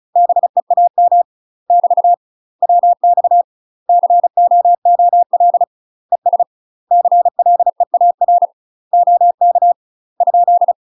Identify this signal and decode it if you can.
Morse code, heard as a keyed tone, about 34 wpm, 700 Hz. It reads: BEAM <BT> WX COOL ES CLEAR OK ?
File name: UK MASTER QSOs TRAINING / 35WPM / UK MASTER QSOs 0035